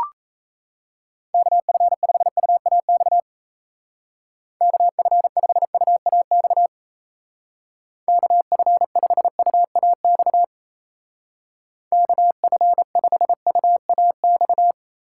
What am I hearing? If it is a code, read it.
Morse code, heard as a keyed tone, about 34 wpm, 700 Hz. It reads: KF5UAX KF5UAX KF5UAX KF5UAX
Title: Speed Racing - USA 2x3 Call Signs 28wpm